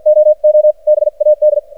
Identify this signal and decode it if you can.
Morse code, heard as a keyed tone, about 54 wpm, spaced as for 44 wpm, 610 Hz. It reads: OO BAD
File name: archived / CRSnet